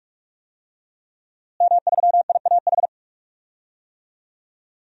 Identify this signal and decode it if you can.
Morse code, heard as a keyed tone, about 46 wpm, 700 Hz. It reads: M3IAH
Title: DXCC-100 Call Signs - 5 Characters - Repeated 2x - 45wpm